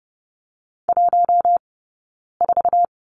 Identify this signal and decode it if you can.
Morse code, heard as a keyed tone, about 30 wpm, spaced as for 9 wpm, 690 Hz. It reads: 14
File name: course / 30:10wpm / 086 Lesson 15d - Number 5 - Two Characters 30wpm